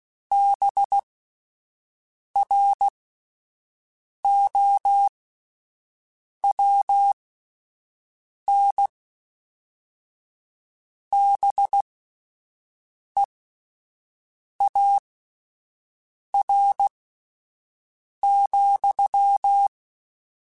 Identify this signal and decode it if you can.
Morse code, heard as a keyed tone, about 16 wpm, spaced as for 5 wpm, 800 Hz. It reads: BROWN BEAR,